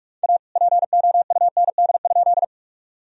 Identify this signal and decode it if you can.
Morse code, heard as a keyed tone, about 46 wpm, 700 Hz. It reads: A POUND?
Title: Sentences from Top 1000 Words - Review Part 2 - 45wpm